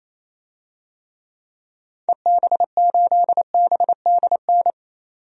Morse code, heard as a keyed tone, about 28 wpm, 700 Hz. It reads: EB8BDN
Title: DXCC-100 Call Signs - 6 Characters - 28wpm